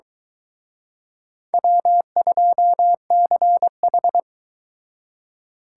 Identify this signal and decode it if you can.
Morse code, heard as a keyed tone, about 23 wpm, 700 Hz. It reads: W2CH